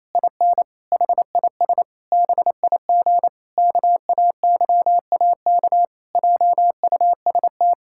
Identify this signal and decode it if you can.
Morse code, heard as a keyed tone, about 27 wpm, 700 Hz. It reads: IN HIS BIG KAYAK JUST
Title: Pangram Challenge v2 - 28wpm